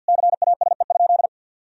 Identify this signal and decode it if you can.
Morse code, heard as a keyed tone, about 50 wpm, 700 Hz. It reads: CASE?
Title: Sentences from Top 1000 Words - Review Part 4 - 50wpm